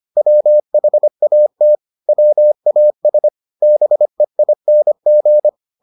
Morse code, heard as a keyed tone, about 25 wpm, 600 Hz. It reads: WHAT WAS BEING